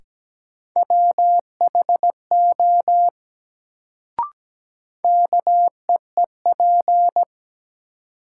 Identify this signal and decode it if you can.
Morse code, heard as a keyed tone, about 17 wpm, 700 Hz. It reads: WHO KEEP
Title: Sets of 2 Words - New Word Drill - 17wpm